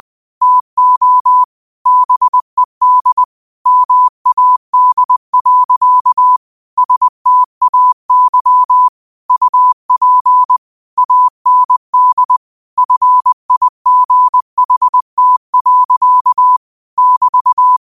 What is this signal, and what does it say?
Morse code, heard as a keyed tone, about 20 wpm, 1000 Hz. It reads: TO BED MAD. STAY UP AND FIGHT. <BT>
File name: qotdfiles / 20wpm